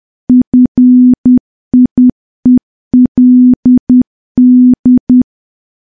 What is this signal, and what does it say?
Morse code, heard as a keyed tone, about 10 wpm, 250 Hz. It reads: FIELD
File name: morse / morse-training-10WPM